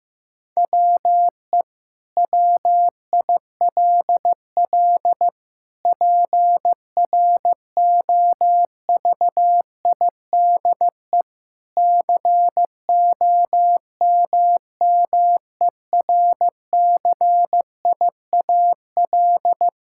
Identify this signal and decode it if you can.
Morse code, heard as a keyed tone, about 15 wpm, 700 Hz. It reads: WE WILL PROVIDE COMMERCIAL
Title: Sentences from Top 1000 Words - Review Part 5 - 15wpm